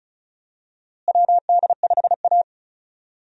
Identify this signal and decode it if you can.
Morse code, heard as a keyed tone, about 36 wpm, 700 Hz. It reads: WD5A
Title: USA Call Signs - 4 Characters - Repeated 3x - 35wpm